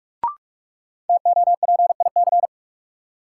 Morse code, heard as a keyed tone, about 46 wpm, 700 Hz. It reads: TOPIC
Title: Sentences from Top 1200 Words - New Word Drill Alt - 45wpm